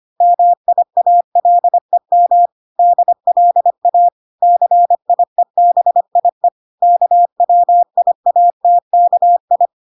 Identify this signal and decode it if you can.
Morse code, heard as a keyed tone, about 25 wpm, 700 Hz. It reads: MIALEM DLA CIEBIE KWIATKI